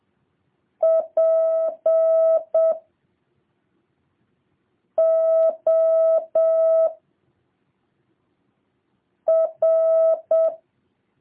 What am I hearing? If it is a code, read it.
Morse code, heard as a keyed tone, about 7 wpm, 650 Hz. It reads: P O R